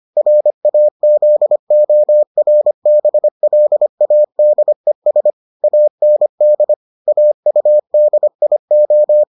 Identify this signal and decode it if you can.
Morse code, heard as a keyed tone, about 25 wpm, 600 Hz. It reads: RAZORBLADES AND AUDIO